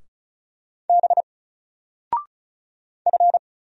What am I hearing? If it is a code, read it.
Morse code, heard as a keyed tone, about 35 wpm, 700 Hz. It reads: B F
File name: Single-Letter-Number-35wpm